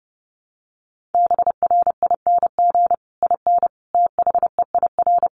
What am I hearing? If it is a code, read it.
Morse code, heard as a keyed tone, about 30 wpm, 700 Hz. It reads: BRING IN THEIR
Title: Sentences from Top 1300 Words 30wpm